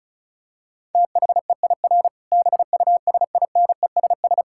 Morse code, heard as a keyed tone, about 36 wpm, 700 Hz. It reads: THEIR BUSINESS
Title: Sentences from Top 1000 Words - Review Part 5 - 35wpm